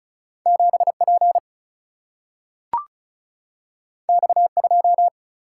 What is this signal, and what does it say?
Morse code, heard as a keyed tone, about 36 wpm, 700 Hz. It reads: 7 P X 2
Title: Single-Letter Single-Number 35wpm